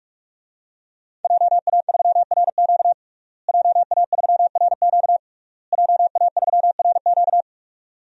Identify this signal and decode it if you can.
Morse code, heard as a keyed tone, about 46 wpm, 700 Hz. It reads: JA3RQ JA3RQ JA3RQ
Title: DXCC-100 Call Signs - 5 Characters - Repeated 3x - 45wpm